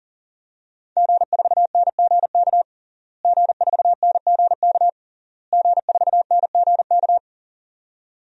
Morse code, heard as a keyed tone, about 40 wpm, 700 Hz. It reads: G4NGK G4NGK G4NGK
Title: DXCC-100 Call Signs - 5 Characters - Repeated 3x - 40wpm